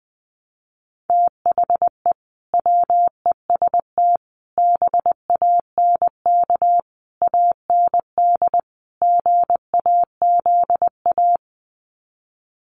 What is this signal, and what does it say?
Morse code, heard as a keyed tone, about 19 wpm, 700 Hz. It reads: THE WEST BANK AND GAZA
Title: Sets of 5 Words 20wpm